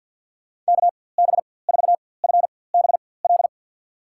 Morse code, heard as a keyed tone, about 51 wpm, spaced as for 23 wpm, 700 Hz. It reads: KB4FBL
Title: USA Call Signs - 6 Characters - 4x Character Spacing - 50wpm